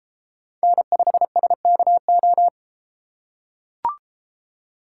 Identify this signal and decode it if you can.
Morse code, heard as a keyed tone, about 31 wpm, 700 Hz. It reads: N5SKO